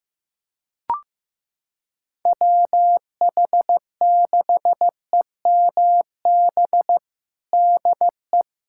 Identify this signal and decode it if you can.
Morse code, heard as a keyed tone, about 15 wpm, 700 Hz. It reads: WH6EMB DE
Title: USA Callsign DE Callsign 15wpm